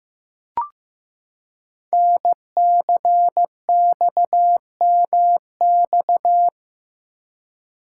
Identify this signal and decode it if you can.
Morse code, heard as a keyed tone, about 15 wpm, 700 Hz. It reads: NCXMX
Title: Code Groups - 5 Letters - 15wpm